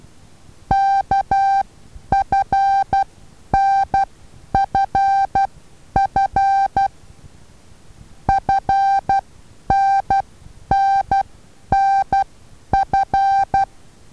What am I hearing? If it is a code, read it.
Morse code, heard as a keyed tone, about 12 wpm, spaced as for 9 wpm, 780 Hz. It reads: KFNFF FNNNF
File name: Taviro HA6VA B